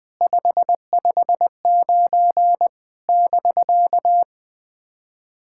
Morse code, heard as a keyed tone, about 20 wpm, 700 Hz. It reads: 559 <BK>
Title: SOTA QSO Practice 20wpm